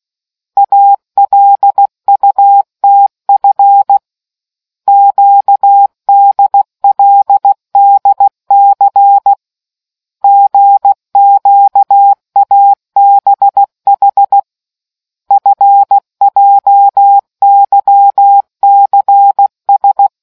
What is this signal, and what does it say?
Morse code, heard as a keyed tone, about 16 wpm, 800 Hz. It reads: ALUTF QDLDC GQABH FJYCS